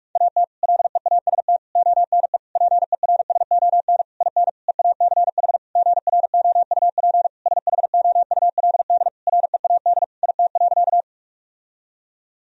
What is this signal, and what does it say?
Morse code, heard as a keyed tone, about 45 wpm, 700 Hz. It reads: AT LEAST ONE PERSON IN EACH GROUP SHOULD READ IT.
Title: Sentences from Top 1000 Words - Review Part 1 - 45wpm